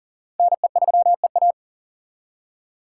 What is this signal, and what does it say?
Morse code, heard as a keyed tone, about 52 wpm, 700 Hz. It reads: NE3EA